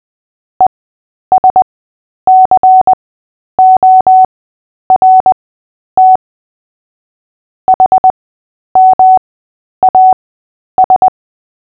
Morse code, heard as a keyed tone, about 20 wpm, spaced as for 10 wpm, 750 Hz. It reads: ESCORT HMAS
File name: cw / fists / Short-Story-3-10wpm